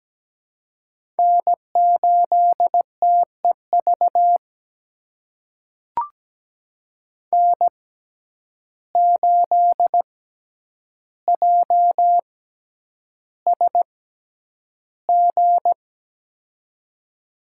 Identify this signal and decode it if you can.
Morse code, heard as a keyed tone, about 17 wpm, spaced as for 9 wpm, 700 Hz. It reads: N8TEV N8JSG
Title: USA Call Signs - 5 Characters - 6x Character Spacing - 17wpm